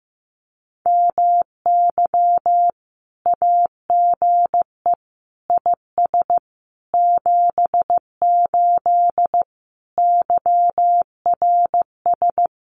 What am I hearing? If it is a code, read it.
Morse code, heard as a keyed tone, about 15 wpm, 700 Hz. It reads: MY AGE IS 78 YRS
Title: Common QSO Phrases Encore 15wpm